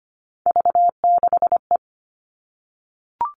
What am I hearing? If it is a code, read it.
Morse code, heard as a keyed tone, about 25 wpm, 700 Hz. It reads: V6E
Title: All Cut Numbers - 3-Digit 25wpm